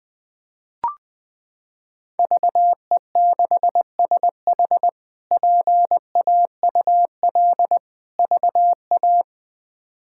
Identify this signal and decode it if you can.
Morse code, heard as a keyed tone, about 21 wpm, 700 Hz. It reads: VE6SH PAUL VA
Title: K1USN SST QSO Practice - 20wpm